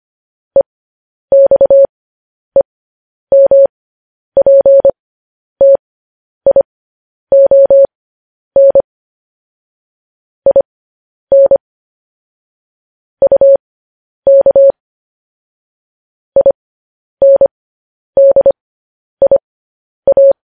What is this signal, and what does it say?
Morse code, heard as a keyed tone, about 25 wpm, spaced as for 10 wpm, 550 Hz. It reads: EXEMPTION IN UK INDIA
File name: podcast / News headlines-1x-25x10